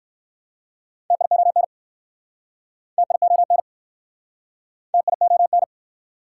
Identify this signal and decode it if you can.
Morse code, heard as a keyed tone, about 60 wpm, 690 Hz. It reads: TION TION TION